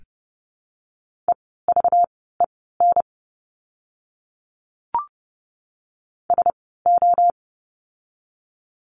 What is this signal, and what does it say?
Morse code, heard as a keyed tone, about 30 wpm, spaced as for 17 wpm, 700 Hz. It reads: EVEN SO
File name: course / 30:10wpm / 230 Lesson 37f - Letter x - Words 30wpm